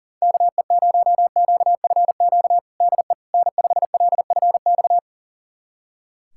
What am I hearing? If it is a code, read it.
Morse code, heard as a keyed tone, about 40 wpm, 700 Hz. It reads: KE0QFQ DE N5LFX
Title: USA Callsign DE Callsign 40wpm